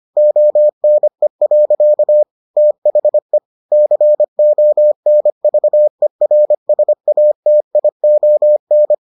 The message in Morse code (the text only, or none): ONE. THE CONVERSATION